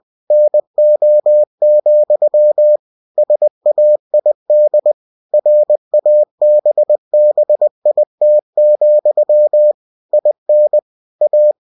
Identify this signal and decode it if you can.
Morse code, heard as a keyed tone, about 20 wpm, 600 Hz. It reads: NO, SAID RABBIT, IN A
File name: cw-books / winnie-the-pooh / winnie-20wpm-page-0014